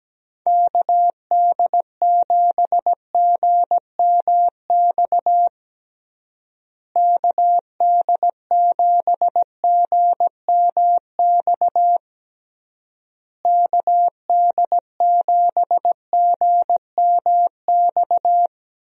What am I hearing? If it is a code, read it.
Morse code, heard as a keyed tone, about 17 wpm, 700 Hz. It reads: KD7GMX KD7GMX KD7GMX
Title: USA Call Signs - 6 Characters - Repeated 3x - 17wpm